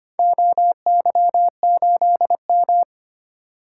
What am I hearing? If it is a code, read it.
Morse code, heard as a keyed tone, about 25 wpm, 700 Hz. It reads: OY8M
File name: practice / 25wpm / Four Letter-Number Combinations - 25wpm